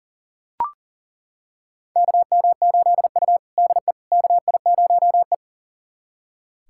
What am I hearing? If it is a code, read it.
Morse code, heard as a keyed tone, about 37 wpm, 700 Hz. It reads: KM8U DE KI0E
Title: USA Callsign DE Callsign - 4 Characters - 40wpm